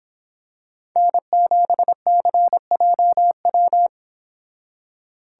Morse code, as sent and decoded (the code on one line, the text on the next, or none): -. --... -.-. .--- .--
N7CJW